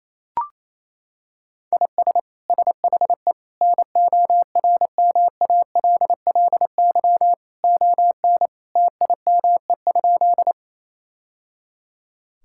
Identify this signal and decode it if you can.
Morse code, heard as a keyed tone, about 28 wpm, 700 Hz. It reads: IS SHE NORMALLY ON TIME?
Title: Sentences from Top 1000 Words - Review Part 6 - 28wpm